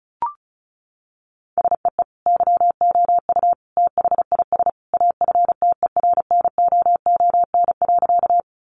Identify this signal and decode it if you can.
Morse code, heard as a keyed tone, about 34 wpm, 700 Hz. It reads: SEE YOU THIS AFTERNOON.
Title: Sentences from Top 1500 Words 35wpm